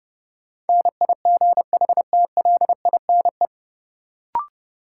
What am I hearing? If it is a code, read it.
Morse code, heard as a keyed tone, about 30 wpm, 700 Hz. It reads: NIGHTLINE
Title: Sets of 5 Words - New Word Drill - 30wpm